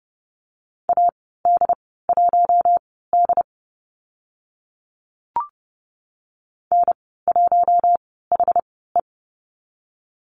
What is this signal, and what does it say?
Morse code, heard as a keyed tone, about 30 wpm, spaced as for 17 wpm, 700 Hz. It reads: AD1D N1HE
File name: Lesson 15g - Number 5 - Call Signs 30wpm